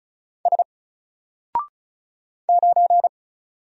Morse code, heard as a keyed tone, about 33 wpm, 700 Hz. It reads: S 9
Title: Single Letter-Number Alt 35wpm